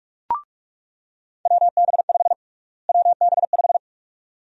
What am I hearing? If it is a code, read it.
Morse code, heard as a keyed tone, about 45 wpm, 700 Hz. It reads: WB5 WB5
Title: USA Call Sign Prefixes - Repeated 2x - 45wpm